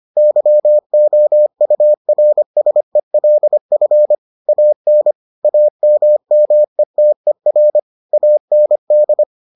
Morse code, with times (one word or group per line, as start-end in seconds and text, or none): 0.2-4.2 YOURSELF
4.5-5.1 AN
5.4-7.8 AMMETER
8.1-9.2 AND